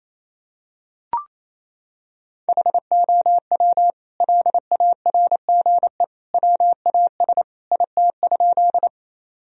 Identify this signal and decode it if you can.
Morse code, heard as a keyed tone, about 28 wpm, 700 Hz. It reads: HOW LARGE WAS IT?
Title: Sentences from Top 1000 Words - Review Part 1 - 28wpm